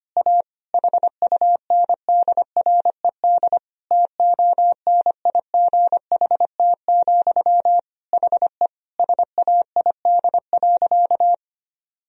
Morse code, heard as a keyed tone, about 25 wpm, 700 Hz. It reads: A HUNDRED TONIGHT, HE SAID.
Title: Sentences from Top 1000 Words - Review Part 2 - 25wpm